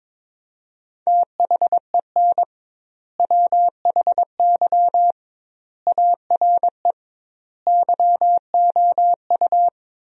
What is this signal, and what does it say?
Morse code, heard as a keyed tone, about 22 wpm, 700 Hz. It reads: THEN WHY ARE YOU
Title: 2x Word-Spacing - Sentences from Top 200 Words - 22wpm